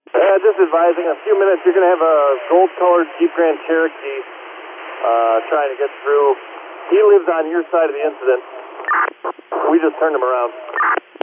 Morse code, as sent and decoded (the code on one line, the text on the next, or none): none